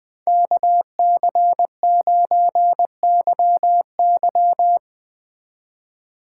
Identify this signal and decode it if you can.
Morse code, heard as a keyed tone, about 20 wpm, 700 Hz. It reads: KC9YY